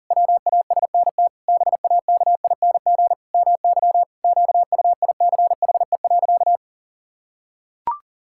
Morse code, heard as a keyed tone, about 37 wpm, 700 Hz. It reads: WASNT BAKING MY QUICHE.
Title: Pangram Challenge v2 - 40wpm